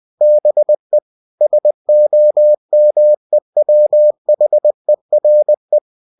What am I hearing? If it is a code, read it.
Morse code, heard as a keyed tone, about 20 wpm, 600 Hz. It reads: BE SOMEWHERE
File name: cw-books / winnie-the-pooh / winnie-20wpm-page-0036